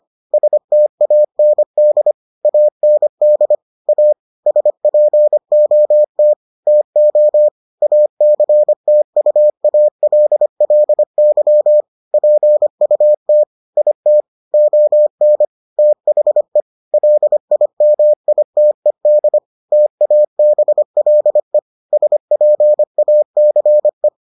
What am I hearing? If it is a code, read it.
Morse code, heard as a keyed tone, about 25 wpm, 600 Hz. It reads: STAND AND A SPOT TO ACTUALLY PUT IT ON THE LIMITED TABLE SPACE